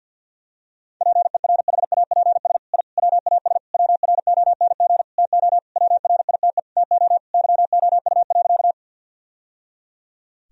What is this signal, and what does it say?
Morse code, heard as a keyed tone, about 52 wpm, 700 Hz. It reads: PERHAPS I WAS WRONG TO WRITE TO YOU.
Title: Sentences from Top 1000 Words - Review Part 1 - 50wpm